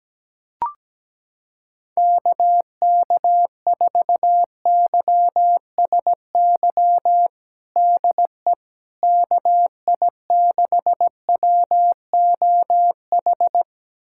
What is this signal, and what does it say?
Morse code, heard as a keyed tone, about 17 wpm, 700 Hz. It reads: KK4YSY DE KI6WOH
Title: USA Callsign DE Callsign 17wpm